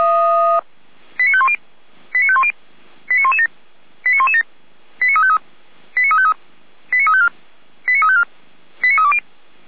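Five-tone selective call (ZVEI), heard as quick runs of five tones, tones of about 70 ms. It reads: 89510 89510 89117 89117 89353 89353 89354 89354 89320
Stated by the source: Received by SDR, ZVEI five-tone selective call.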